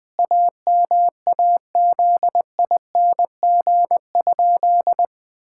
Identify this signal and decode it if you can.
Morse code, heard as a keyed tone, about 20 wpm, 700 Hz. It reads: AMAZING?
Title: Pangram Challenge v2 - 20wpm